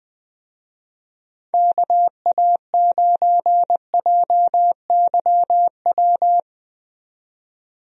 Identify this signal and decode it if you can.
Morse code, heard as a keyed tone, about 20 wpm, 700 Hz. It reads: KA9JYW
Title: USA Call Signs - 6 Characters - 20wpm